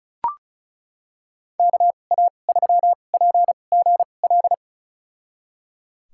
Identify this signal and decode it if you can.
Morse code, heard as a keyed tone, about 36 wpm, spaced as for 25 wpm, 700 Hz. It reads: KA3PGL
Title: USA Call Signs - 6 Characters - 2x Character Spacing - 35wpm